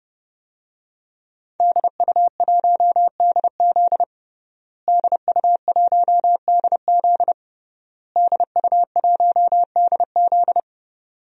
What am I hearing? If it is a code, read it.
Morse code, heard as a keyed tone, about 32 wpm, 700 Hz. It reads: DU1DZ DU1DZ DU1DZ
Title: DXCC-100 Call Signs - 5 Characters - Repeated 3x - 30wpm